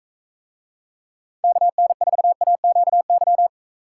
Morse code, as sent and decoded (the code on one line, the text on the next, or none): -.- -. ....- .- --.- -.--
KN4AQY